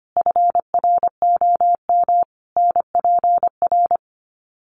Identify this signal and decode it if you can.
Morse code, heard as a keyed tone, about 25 wpm, 700 Hz. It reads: FROM NPR
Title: Sets of 5 Words Encore 25wpm